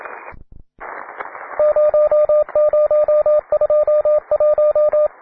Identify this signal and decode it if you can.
Morse code, heard as a keyed tone, about 27 wpm, 600 Hz, with radio noise. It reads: 0021